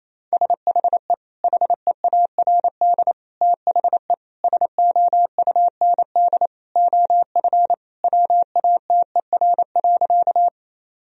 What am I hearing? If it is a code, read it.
Morse code, heard as a keyed tone, about 28 wpm, 700 Hz. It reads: SHE HEARD THE SOUND OF WATER.
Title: Sentences from Top 1000 Words - Review Part 1 - 28wpm